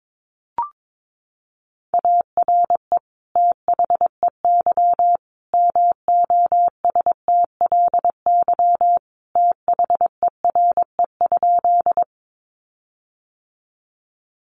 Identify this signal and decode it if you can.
Morse code, heard as a keyed tone, about 22 wpm, 700 Hz. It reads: ARE THEY MOSTLY THERE?
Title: Sentences from Top 1300 Words 22wpm